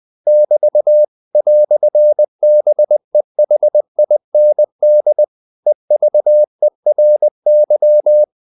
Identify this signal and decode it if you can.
Morse code, heard as a keyed tone, about 20 wpm, 600 Hz. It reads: <BT> "BEHIND EVERY
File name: CW / Practice / EnglishQuotations0320